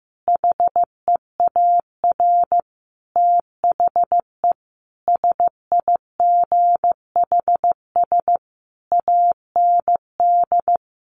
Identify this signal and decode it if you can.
Morse code, heard as a keyed tone, about 15 wpm, 700 Hz. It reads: HEAR THE SIGHS AND
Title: A Sound Of Thunder 15wpm